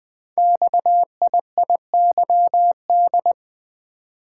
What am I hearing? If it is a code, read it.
Morse code, heard as a keyed tone, about 20 wpm, 700 Hz. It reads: XIIYD
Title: Code Groups - 5 Letter:Numbers - 20wpm